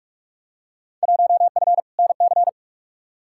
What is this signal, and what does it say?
Morse code, heard as a keyed tone, about 46 wpm, 700 Hz. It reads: 1F NC
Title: Field Day Exchanges - 45wpm